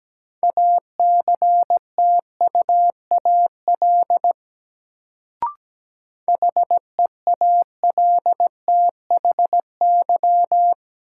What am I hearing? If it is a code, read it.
Morse code, heard as a keyed tone, about 17 wpm, 700 Hz. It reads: ACTUAL HEALTHY
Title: Sentences from Top 1000 Words - Review Part 5 - New Word Drill - 17wpm